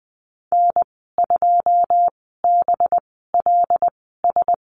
Morse code, heard as a keyed tone, about 20 wpm, spaced as for 15 wpm, 700 Hz. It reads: N2BLS
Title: 219 Lesson 35g - Number 0 - Call Signs 20wpm